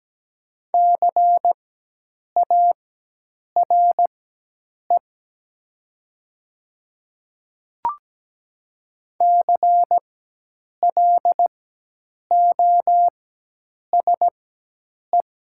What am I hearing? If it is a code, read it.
Morse code, heard as a keyed tone, about 17 wpm, spaced as for 8 wpm, 700 Hz. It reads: CARE CLOSE
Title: 4x Character Spacing - Top 300 Words - 17wpm